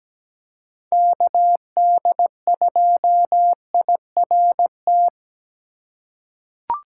Morse code, heard as a keyed tone, about 17 wpm, 700 Hz. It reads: KD2IRT
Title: USA Call Signs - 6 Characters - 3x Character Spacing - 17wpm